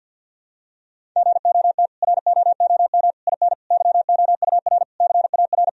Morse code, heard as a keyed tone, about 51 wpm, 700 Hz. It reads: GOT ROOM IN YOUR CAR
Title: Sentences from Top 1000 Words - Review Part 1 - 50wpm